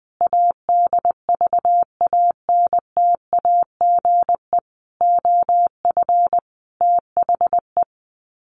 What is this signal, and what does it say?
Morse code, heard as a keyed tone, about 20 wpm, 700 Hz. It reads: ADVANTAGE OF THE
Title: Sets of 5 Words 20wpm